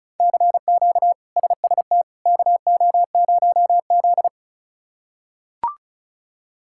Morse code, heard as a keyed tone, about 34 wpm, 700 Hz. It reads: CQ SST KO0Z
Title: K1USN SST QSO Practice - 35wpm